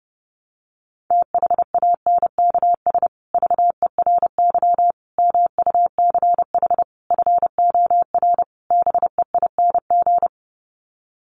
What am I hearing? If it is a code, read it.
Morse code, heard as a keyed tone, about 28 wpm, 700 Hz. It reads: THANKS VERY MUCH FOR BEING
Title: Sets of 5 Words Encore 30wpm